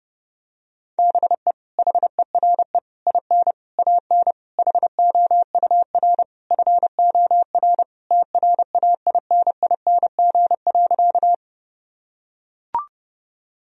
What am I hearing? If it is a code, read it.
Morse code, heard as a keyed tone, about 30 wpm, 700 Hz. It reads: BE HERE IN AN HOUR FOR TRAINING.
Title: Sentences from Top 1000 Words - Review Part 4 - 30wpm